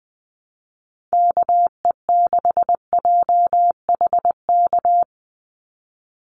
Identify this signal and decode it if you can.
Morse code, heard as a keyed tone, about 20 wpm, 700 Hz. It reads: KE6JHK
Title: USA Call Signs 20wpm